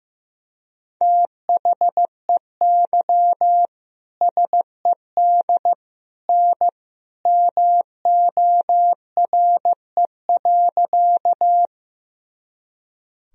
Morse code, heard as a keyed tone, about 15 wpm, 700 Hz. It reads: THEY SED N MORE.
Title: Sentences from Top 100 Words and Abbreviations 15wpm